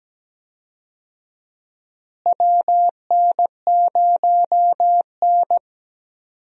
Morse code, heard as a keyed tone, about 17 wpm, 700 Hz. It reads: WN0N
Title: USA Call Signs - 4 Characters - Repeated 2x - 17wpm